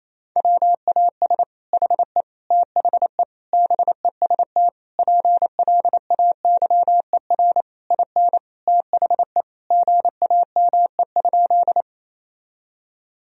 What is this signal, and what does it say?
Morse code, heard as a keyed tone, about 28 wpm, 700 Hz. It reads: WAS HE THE BEST PLAYER IN THE GAME?
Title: Sentences from Top 1000 Words - Review Part 4 - 28wpm